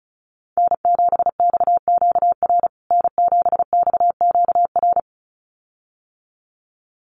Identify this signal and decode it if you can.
Morse code, heard as a keyed tone, about 34 wpm, 700 Hz. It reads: N7XQR N7XQR
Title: USA Call Signs Repeated 2x 35wpm